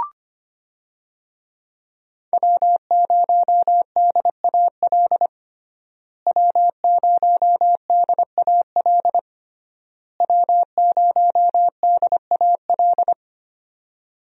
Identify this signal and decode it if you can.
Morse code, heard as a keyed tone, about 25 wpm, 700 Hz. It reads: W0DAL W0DAL W0DAL
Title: USA Call Signs - 5 Characters - Repeated 3x - 25wpm